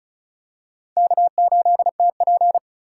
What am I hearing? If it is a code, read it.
Morse code, heard as a keyed tone, about 36 wpm, 700 Hz. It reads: K8TP
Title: USA Call Signs - 3x Character Spacing - 35wpm